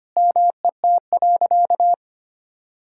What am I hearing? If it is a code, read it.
Morse code, heard as a keyed tone, about 25 wpm, 700 Hz. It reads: MET.
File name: practice / 25wpm / Sentences with 4-Word Phrases - 25wpm